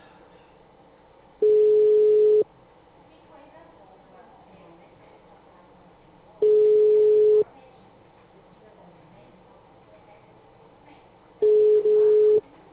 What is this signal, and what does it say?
Ringback tone (Europe, 425 Hz), with speech.
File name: out- -2001- - - .7716